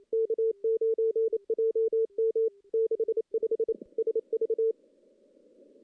Morse code, heard as a keyed tone, about 28 wpm, 410 Hz. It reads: K9JM 65 SV
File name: contests / CW-SS-k5tr- -1600